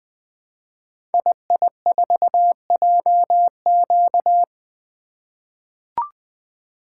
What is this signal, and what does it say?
Morse code, heard as a keyed tone, about 20 wpm, 700 Hz. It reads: II4JQ
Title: Code Groups - 5 Letter:Numbers - 20wpm